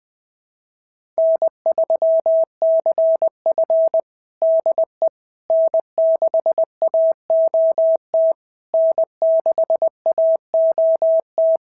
Morse code, heard as a keyed tone, about 20 wpm, 690 Hz. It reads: N3CF DE N6AOT N6AOT
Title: SOTA QSO Practice 20wpm